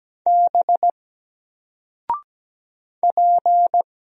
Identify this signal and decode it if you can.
Morse code, heard as a keyed tone, about 17 wpm, 700 Hz. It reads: B P
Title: Single Letter-Number Alt 17wpm